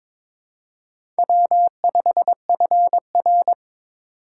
Morse code, heard as a keyed tone, about 23 wpm, 700 Hz. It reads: W5FR
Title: USA Call Signs - 4 Characters - 4x Character Spacing - 22wpm